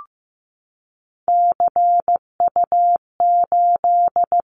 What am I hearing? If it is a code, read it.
Morse code, heard as a keyed tone, about 15 wpm, 700 Hz. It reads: CU8
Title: Callsign Prefixes for Top 50 DXCC - 15wpm